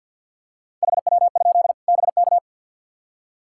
Morse code, heard as a keyed tone, about 52 wpm, 700 Hz. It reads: HW? BK
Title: K1USN SST QSO Practice 50wpm